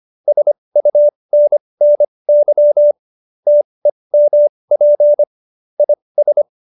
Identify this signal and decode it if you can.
Morse code, heard as a keyed tone, about 25 wpm, spaced as for 20 wpm, 590 Hz. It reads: SUNNY TEMP IS
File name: QSO307 20